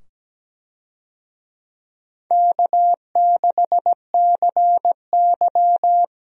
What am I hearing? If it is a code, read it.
Morse code, heard as a keyed tone, about 17 wpm, 700 Hz. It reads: K6CY
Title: USA Call Signs - 4 Characters - 17wpm